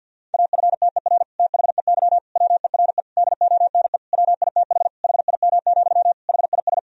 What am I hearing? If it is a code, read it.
Morse code, heard as a keyed tone, about 50 wpm, 700 Hz. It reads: AFTER THEY WERE DONE WITH HIM, HIS
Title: Sentences from Top 1000 Words - Review Part 2 - 50wpm